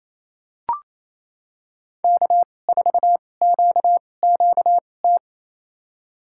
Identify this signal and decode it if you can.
Morse code, heard as a keyed tone, about 28 wpm, spaced as for 20 wpm, 700 Hz. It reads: K4QQT
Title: USA Call Signs - 5 Characters - 2x Character Spacing - 28wpm